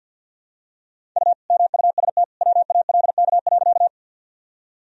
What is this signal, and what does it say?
Morse code, heard as a keyed tone, about 48 wpm, 700 Hz. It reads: U MUST WALK.
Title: Sentences from Top 300 Words and Abbreviations 50wpm